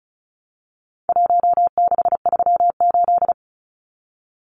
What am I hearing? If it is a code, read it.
Morse code, heard as a keyed tone, about 34 wpm, 700 Hz. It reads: 1638
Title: Numbers - Contest Style - 4-Digit 35wpm